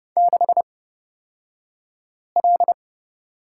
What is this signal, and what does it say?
Morse code, heard as a keyed tone, about 30 wpm, spaced as for 5 wpm, 700 Hz. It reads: 6L
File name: Lesson 37d - Letter x - Two Characters 30wpm